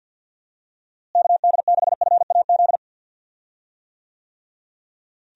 Morse code, heard as a keyed tone, about 51 wpm, 700 Hz. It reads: KD6FAZ
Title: USA Call Signs - 6 Characters - Repeated 2x - 50wpm